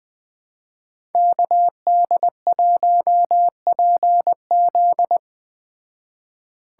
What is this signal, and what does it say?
Morse code, heard as a keyed tone, about 20 wpm, 700 Hz. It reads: KD1PZ
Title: Speed Racing - USA 2x2 Call Signs 17wpm